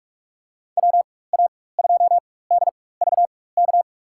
Morse code, heard as a keyed tone, about 46 wpm, spaced as for 21 wpm, 700 Hz. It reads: WA2DVK
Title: USA Call Signs - 6 Characters - 4x Character Spacing - 45wpm